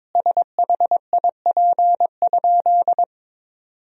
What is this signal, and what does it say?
Morse code, heard as a keyed tone, about 22 wpm, 700 Hz. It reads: SHIP?
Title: 2x Word-Spacing - Sentences from Top 400 Words - 22wpm